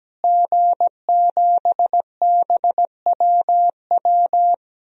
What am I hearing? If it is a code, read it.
Morse code, heard as a keyed tone, about 17 wpm, 700 Hz. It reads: G7BWW
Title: DXCC-100 Call Signs - 5 Characters - Repeated 2x - 17wpm